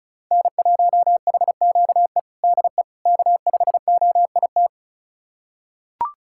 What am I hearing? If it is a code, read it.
Morse code, heard as a keyed tone, about 36 wpm, 700 Hz. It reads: N1HQE DE K5OIT
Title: USA Callsign DE Callsign - 5 Characters - 35wpm